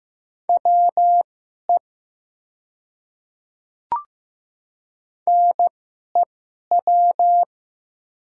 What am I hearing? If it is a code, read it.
Morse code, heard as a keyed tone, about 15 wpm, spaced as for 11 wpm, 700 Hz. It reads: WE NEW
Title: 2x Character Spacing - Top 200 Words - 15wpm